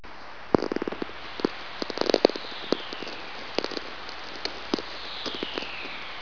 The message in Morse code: none